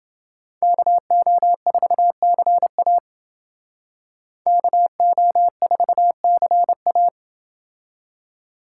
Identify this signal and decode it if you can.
Morse code, heard as a keyed tone, about 29 wpm, 700 Hz. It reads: KO4CA KO4CA